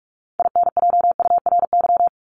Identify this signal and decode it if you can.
Morse code, heard as a keyed tone, about 44 wpm, 700 Hz. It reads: INJURY
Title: Top 1751-2000 Words 45wpm